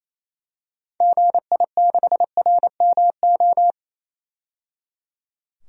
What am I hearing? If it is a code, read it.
Morse code, heard as a keyed tone, about 28 wpm, 700 Hz. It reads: GI6RMO
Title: DXCC-100 Call Signs - 6 Characters - 28wpm